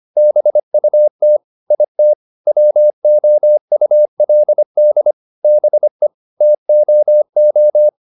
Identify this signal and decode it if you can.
Morse code, heard as a keyed tone, about 25 wpm, 600 Hz. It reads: BUT IT WOULD BE TOO